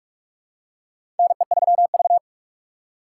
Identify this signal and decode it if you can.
Morse code, heard as a keyed tone, about 43 wpm, 700 Hz. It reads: NE3V
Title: USA Call Signs - 4 Characters - 6x Character Spacing - 45wpm